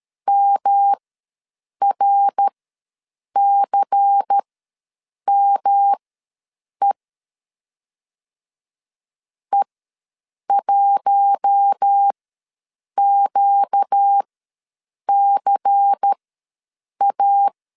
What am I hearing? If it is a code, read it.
Morse code, heard as a keyed tone, about 13 wpm, spaced as for 7 wpm, 800 Hz. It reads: MRCME E1QCA
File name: LEZIONI / 5 LBVC / 5 LBVC ws 12wpm 3159